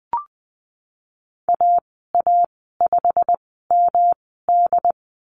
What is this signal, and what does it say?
Morse code, heard as a keyed tone, about 20 wpm, spaced as for 15 wpm, 700 Hz. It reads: AA5MD
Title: 128 Lesson 21g - Number 6 - Call Signs 20wpm